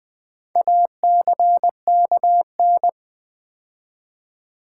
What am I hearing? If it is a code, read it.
Morse code, heard as a keyed tone, about 20 wpm, 700 Hz. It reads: ACKN
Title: Build A Word - 4 Letters At A Time - 11-Letter Words - 20wpm